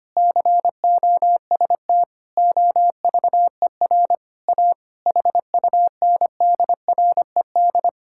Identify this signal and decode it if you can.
Morse code, heard as a keyed tone, about 25 wpm, 700 Hz. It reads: COST OVER A HUNDRED
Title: Sentences from Top 1000 Words - Review Part 5 - 25wpm